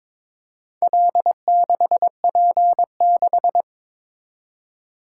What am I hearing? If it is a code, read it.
Morse code, heard as a keyed tone, about 22 wpm, 700 Hz. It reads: L6P6
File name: practice / 22wpm / Four Letter-Number Combinations - 22wpm